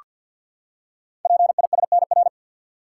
Morse code, heard as a keyed tone, about 46 wpm, 700 Hz. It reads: PISNR
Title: Code Groups - 5 Letters - 50wpm